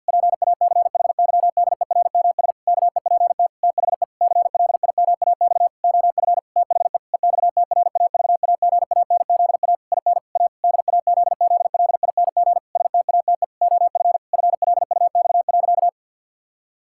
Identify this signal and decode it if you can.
Morse code, heard as a keyed tone, about 51 wpm, 700 Hz. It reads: PACHYDERMS KEPT THE CLIMAX OF THE EXTRAVAGANZA IN A DAZZLING STATE OF FLUX.
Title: Pangram Challenge v2 - 50wpm